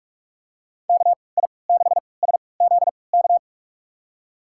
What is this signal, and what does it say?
Morse code, heard as a keyed tone, about 46 wpm, spaced as for 26 wpm, 700 Hz. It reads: KI6SZK
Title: USA Call Signs - 6 Characters - 3x Character Spacing - 45wpm